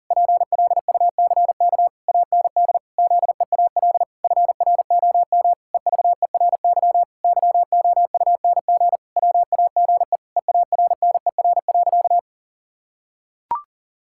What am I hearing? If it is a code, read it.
Morse code, heard as a keyed tone, about 40 wpm, 700 Hz. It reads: PLUCK AND ZEAL FROM EVERY YOUNG WAGE EARNER.
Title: Pangram Challenge v2 - 40wpm